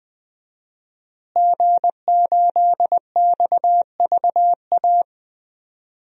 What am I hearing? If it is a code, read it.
Morse code, heard as a keyed tone, about 20 wpm, 700 Hz. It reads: G8XVA
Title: DXCC-100 Call Signs - 5 Characters - Repeated 2x - 20wpm